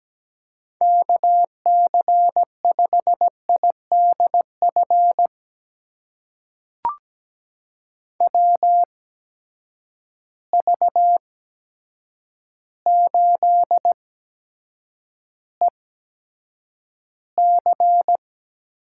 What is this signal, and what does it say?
Morse code, heard as a keyed tone, about 17 wpm, spaced as for 8 wpm, 700 Hz. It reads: KC5IDF WV8EC
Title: USA Call Signs - 6 Characters - 8x Character Spacing - 17wpm